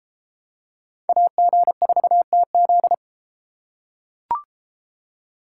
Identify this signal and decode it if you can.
Morse code, heard as a keyed tone, about 30 wpm, 700 Hz. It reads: AG4TZ